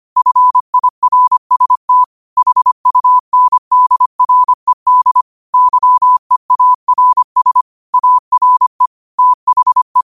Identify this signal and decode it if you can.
Morse code, heard as a keyed tone, about 25 wpm, 1000 Hz. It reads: FIRST HUNDRED YEARS ARE THE
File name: qotdfiles / 25wpm